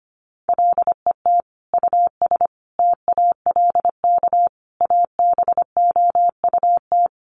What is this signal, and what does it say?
Morse code, heard as a keyed tone, about 25 wpm, 700 Hz. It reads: LET US TALK ABOUT
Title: Sentences from Top 400 Words 25wpm